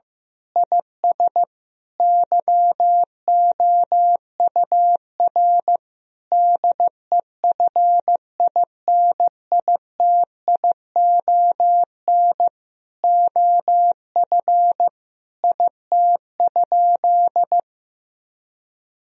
Morse code, heard as a keyed tone, about 15 wpm, 700 Hz. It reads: IS YOUR DEFINITION OF IT?
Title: Sentences from Top 1000 Words - Review Part 4 - 15wpm